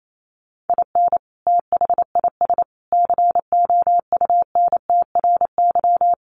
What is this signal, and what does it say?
Morse code, heard as a keyed tone, about 28 wpm, 700 Hz. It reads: IN THIS COUNTRY
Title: Sets of 3 Words 28wpm